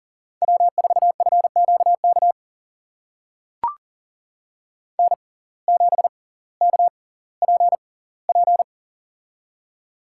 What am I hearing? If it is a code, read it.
Morse code, heard as a keyed tone, about 40 wpm, spaced as for 20 wpm, 700 Hz. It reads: W4FQK N7KPP
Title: USA Call Signs - 5 Characters - 6x Character Spacing - 40wpm